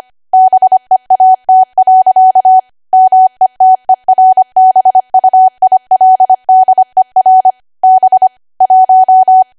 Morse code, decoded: BEAT. METERBUILDER B 1